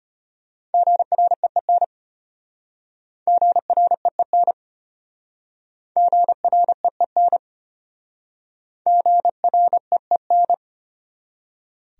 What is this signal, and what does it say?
Morse code, heard as a keyed tone, about 33 wpm, 700 Hz. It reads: GREEN GREEN GREEN GREEN